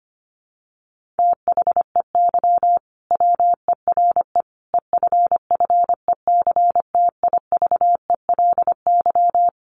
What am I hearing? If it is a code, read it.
Morse code, heard as a keyed tone, about 25 wpm, 700 Hz. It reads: THEY WERE EFFECTIVELY
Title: Sentences from Top 1400 Words 25wpm